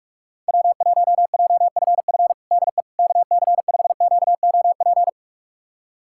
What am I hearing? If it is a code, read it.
Morse code, heard as a keyed tone, about 46 wpm, 700 Hz. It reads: W1JFF DE KC5QOP
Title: USA Callsign DE Callsign 45wpm